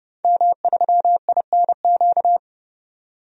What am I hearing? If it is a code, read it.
Morse code, heard as a keyed tone, about 30 wpm, 700 Hz. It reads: M3INQ
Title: DXCC-100 Call Signs - 5 Characters - 30wpm